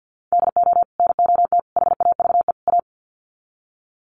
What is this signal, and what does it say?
Morse code, heard as a keyed tone, about 50 wpm, 700 Hz. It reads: DO NOT HAVE A
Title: Sets of 4 Words 50wpm